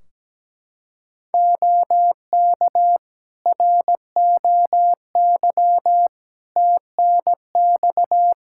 Morse code, heard as a keyed tone, about 17 wpm, 700 Hz. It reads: OK ROY TNX
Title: Practice QSO - Exchange 4 - Ending - 17wpm